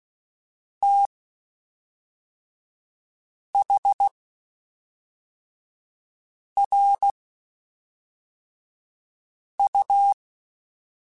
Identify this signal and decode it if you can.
Morse code, heard as a keyed tone, about 16 wpm, spaced as for 3 wpm, 800 Hz. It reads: THRU
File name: (spaced) / QSOs Spaced-mp3s / WD0FBT 1 @5 sp